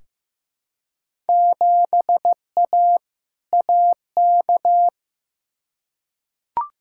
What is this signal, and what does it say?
Morse code, heard as a keyed tone, about 15 wpm, 700 Hz. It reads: 7A AK
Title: Field Day Exchanges Encore - 15wpm